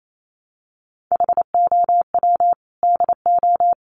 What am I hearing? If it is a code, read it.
Morse code, heard as a keyed tone, about 28 wpm, 700 Hz. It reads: HOW DO
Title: Sets of 3 Words 28wpm